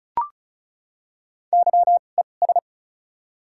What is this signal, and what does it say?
Morse code, heard as a keyed tone, about 36 wpm, spaced as for 25 wpm, 700 Hz. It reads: YES
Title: 2x Character Spacing - Top 500 Words - 35wpm